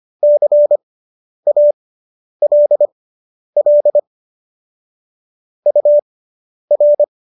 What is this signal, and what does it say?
Morse code, heard as a keyed tone, about 25 wpm, spaced as for 10 wpm, 590 Hz. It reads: CALL UR